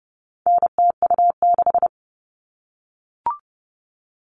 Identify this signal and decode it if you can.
Morse code, heard as a keyed tone, about 30 wpm, 700 Hz. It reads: NTU6
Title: All Cut Numbers - 4-Digit 30wpm